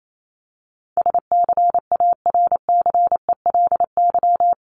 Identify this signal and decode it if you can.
Morse code, heard as a keyed tone, about 28 wpm, 700 Hz. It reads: SCARCELY
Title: LY Suffix - Top 100 Words with Base Words 28wpm